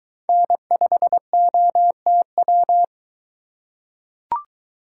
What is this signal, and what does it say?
Morse code, heard as a keyed tone, about 23 wpm, 700 Hz. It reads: N5OTW